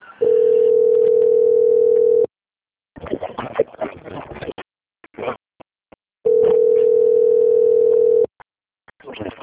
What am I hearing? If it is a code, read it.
Ringback tone (North America), 440 + 480 Hz.